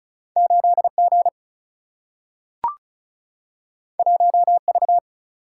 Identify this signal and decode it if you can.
Morse code, heard as a keyed tone, about 36 wpm, 700 Hz. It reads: 8 G 1 V
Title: Single-Letter Single-Number 35wpm